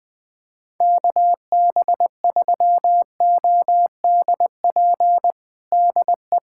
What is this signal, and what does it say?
Morse code, heard as a keyed tone, about 21 wpm, 700 Hz. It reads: KB3ODP DE
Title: USA Callsign DE Callsign 20wpm